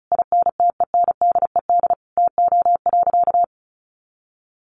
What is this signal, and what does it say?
Morse code, heard as a keyed tone, about 34 wpm, 700 Hz. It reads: INTENDED TO.
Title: A Sound Of Thunder 35wpm